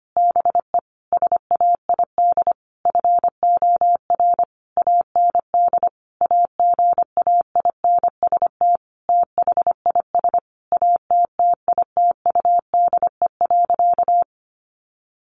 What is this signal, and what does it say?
Morse code, heard as a keyed tone, about 25 wpm, 700 Hz. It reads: BE SAID FOR AND AGAINST THIS ATTITUDE.
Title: Sentences from Top 1300 Words 25wpm